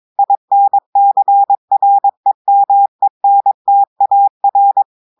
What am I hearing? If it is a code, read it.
Morse code, heard as a keyed tone, about 22 wpm, 800 Hz. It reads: INCREMENTAR
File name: audios / portugais / portugais texte en clair 2 vitesse 22 mots minutes